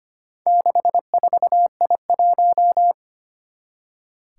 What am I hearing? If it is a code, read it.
Morse code, heard as a keyed tone, about 25 wpm, 700 Hz. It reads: 64I1
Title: Four Challenging Letter-Number Combinations 25wpm